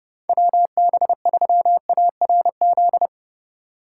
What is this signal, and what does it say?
Morse code, heard as a keyed tone, about 30 wpm, 700 Hz. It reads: WB3ARZ
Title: USA Call Signs - 6 Characters - 2x Character Spacing - 30wpm